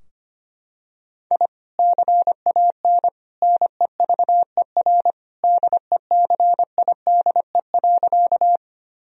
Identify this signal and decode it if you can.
Morse code, heard as a keyed tone, about 25 wpm, 700 Hz. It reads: I CAN NEVER DECIDE.
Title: Sentences from Top 1000 Words - Review Part 3 - 25wpm